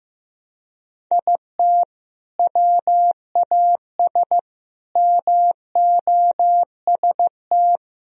Morse code, heard as a keyed tone, about 15 wpm, 700 Hz. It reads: IT WAS MOST
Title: Sentences from Top 1000 Words - Review Part 1 - 15wpm